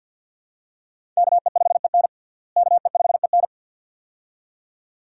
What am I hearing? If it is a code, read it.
Morse code, heard as a keyed tone, about 48 wpm, 700 Hz. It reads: KE5EN KE5EN
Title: USA Call Signs - 5 Characters - Repeated 2x - 50wpm